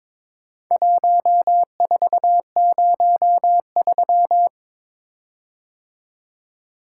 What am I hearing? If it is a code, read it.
Morse code, heard as a keyed tone, about 23 wpm, 700 Hz. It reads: 1403
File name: practice / 22wpm / Four Letter-Number Combinations - 22wpm